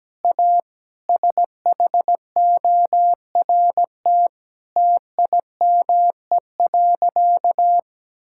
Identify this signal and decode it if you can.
Morse code, heard as a keyed tone, about 17 wpm, 700 Hz. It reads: A SHORT TIME.
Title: Sentences from Top 1600 Words 17wpm